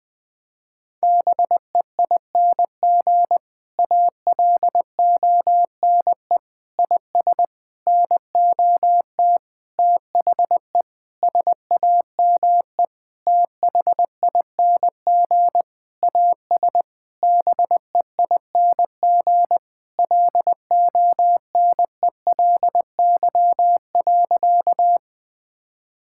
Morse code, heard as a keyed tone, about 20 wpm, 700 Hz. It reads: BEING ALONE IS NOT THE SAME THING AS BEING LONELY.
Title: Sentences with 6-Word Phrases - 20wpm